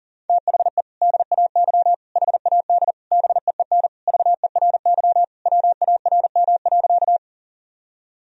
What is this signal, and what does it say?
Morse code, heard as a keyed tone, about 37 wpm, 700 Hz. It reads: THE DAY HAD BEEN VERY WARM.
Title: Sentences from Top 1000 Words - Review Part 2 - 40wpm